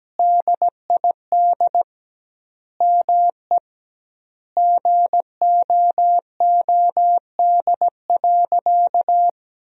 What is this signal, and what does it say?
Morse code, heard as a keyed tone, about 17 wpm, 700 Hz. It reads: DID ME GOOD.
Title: 2x Word-Spacing - Sentences from Top 200 Words - 17wpm